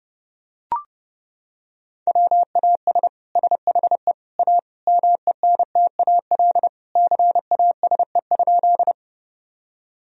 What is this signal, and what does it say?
Morse code, heard as a keyed tone, about 30 wpm, 700 Hz. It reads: WAS SHE A MENTAL CASE?
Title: Sentences from Top 1000 Words - Review Part 4 - 30wpm